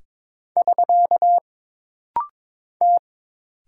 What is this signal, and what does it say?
Morse code, heard as a keyed tone, about 22 wpm, 700 Hz. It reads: <SK> T
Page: This rapid-fire timing is geared to push your ICR (Instant Character Recognition) skills.